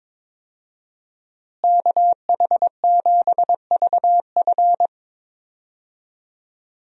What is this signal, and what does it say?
Morse code, heard as a keyed tone, about 21 wpm, 700 Hz. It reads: KH7VF
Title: USA Call Signs - 5 Characters - Repeated 2x - 22wpm